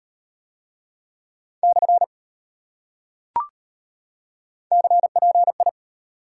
Morse code, heard as a keyed tone, about 39 wpm, 700 Hz. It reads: / CPI